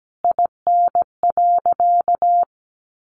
Morse code, heard as a keyed tone, about 17 wpm, 700 Hz. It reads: IN.
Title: Sentences from Top 1300 Words 17wpm